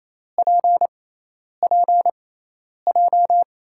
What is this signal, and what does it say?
Morse code, heard as a keyed tone, about 28 wpm, spaced as for 10 wpm, 700 Hz. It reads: PPJ
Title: USA Call Signs - 5 Characters - 6x Character Spacing - 28wpm